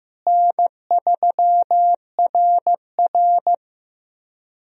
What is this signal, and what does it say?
Morse code, heard as a keyed tone, about 16 wpm, 700 Hz. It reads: N3RR
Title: USA Call Signs - 4 Characters - Repeated 3x - 15wpm